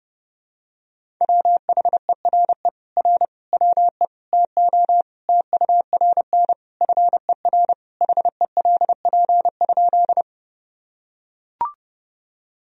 Morse code, heard as a keyed tone, about 30 wpm, 700 Hz. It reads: WHERE R WE TO TURN FER HELP?
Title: Sentences from Top 200 Words and Abbreviations 30wpm